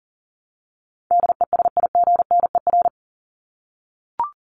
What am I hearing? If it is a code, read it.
Morse code, heard as a keyed tone, about 40 wpm, 700 Hz. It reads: DESIGNER
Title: Top 1501-1750 Words 40wpm